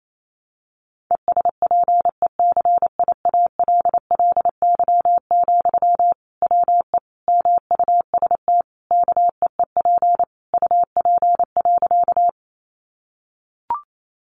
Morse code, heard as a keyed tone, about 28 wpm, 700 Hz. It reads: ESPECIALLY, WE MUST KEEP UP.
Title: Sentences from Top 700 Words 28wpm